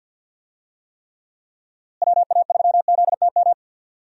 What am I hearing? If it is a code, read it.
Morse code, heard as a keyed tone, about 51 wpm, 700 Hz. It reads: WA3ZTM
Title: USA Call Signs - 6 Characters - Repeated 2x - 50wpm